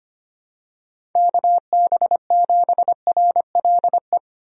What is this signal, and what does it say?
Morse code, heard as a keyed tone, about 25 wpm, 700 Hz. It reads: KB7RLE
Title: USA Call Signs - 6 Characters - 25wpm